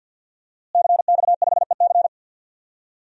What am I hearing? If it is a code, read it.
Morse code, heard as a keyed tone, about 52 wpm, 700 Hz. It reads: CX5EC
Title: DXCC-100 Call Signs - 5 Characters - Repeated 3x - 50wpm